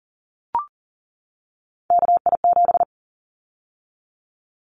Morse code, heard as a keyed tone, about 41 wpm, 700 Hz. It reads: KI7
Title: Callsign Prefixes for Top 25 DXCC - 40wpm